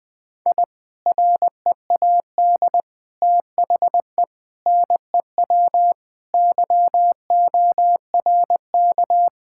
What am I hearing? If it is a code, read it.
Morse code, heard as a keyed tone, about 20 wpm, 700 Hz. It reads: I READ THE NEW YORK
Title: Sentences with 3-Word Phrases - Spoken Beginning - 20wpm